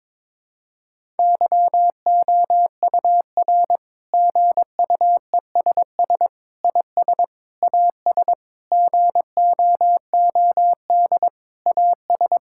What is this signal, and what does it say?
Morse code, heard as a keyed tone, about 22 wpm, 700 Hz. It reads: YOUR GUESS IS AS GOOD AS
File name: Sentences with 7-Word Phrases - 22wpm